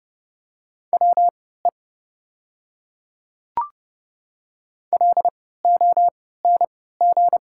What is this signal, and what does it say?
Morse code, heard as a keyed tone, about 30 wpm, spaced as for 17 wpm, 700 Hz. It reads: WE LONG
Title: 3x Character Spacing - Top 100 Words - 30wpm